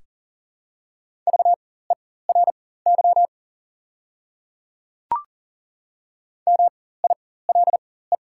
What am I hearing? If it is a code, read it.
Morse code, heard as a keyed tone, about 40 wpm, spaced as for 19 wpm, 700 Hz. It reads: VERY MILE
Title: 4x Character Spacing - Top 500 Words - 40wpm